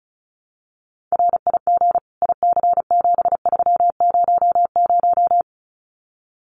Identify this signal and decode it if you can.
Morse code, heard as a keyed tone, about 34 wpm, 700 Hz. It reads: RIG IC7300
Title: Common QSO Phrases Encore 35wpm